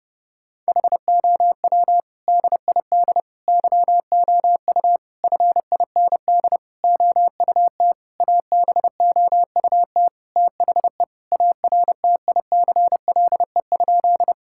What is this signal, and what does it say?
Morse code, heard as a keyed tone, about 30 wpm, 700 Hz. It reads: HOW DID YOU FIND OUT ABOUT THE ARTICLE?
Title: Sentences from Top 1000 Words - Review Part 5 - 30wpm